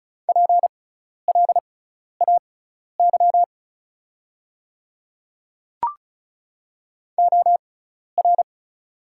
Morse code, heard as a keyed tone, about 34 wpm, spaced as for 12 wpm, 700 Hz. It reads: PLAY OR
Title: 6x Character Spacing - Top 200 Words - 35wpm